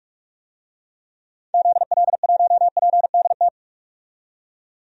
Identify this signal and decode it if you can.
Morse code, heard as a keyed tone, about 42 wpm, 700 Hz. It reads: ZL1PDT
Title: DXCC-100 Call Signs - 6 Characters - 45wpm